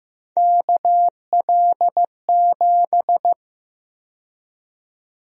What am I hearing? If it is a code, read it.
Morse code, heard as a keyed tone, about 15 wpm, 700 Hz. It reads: KL7
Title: USA Call Sign Prefixes - Repeated 2x - 15wpm